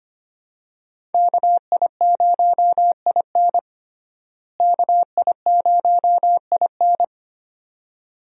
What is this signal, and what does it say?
Morse code, heard as a keyed tone, about 25 wpm, 700 Hz. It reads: KI0IN KI0IN
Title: USA Call Signs - 5 Characters - Repeated 2x - 25wpm